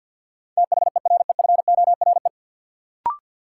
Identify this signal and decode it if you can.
Morse code, heard as a keyed tone, about 51 wpm, 700 Hz. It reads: THEREFORE
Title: Sentences from Top 800 Words - New Word Drill - 50wpm